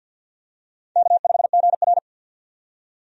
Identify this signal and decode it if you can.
Morse code, heard as a keyed tone, about 53 wpm, 700 Hz. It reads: K5GR
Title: USA Call Signs - 4 Characters - 50wpm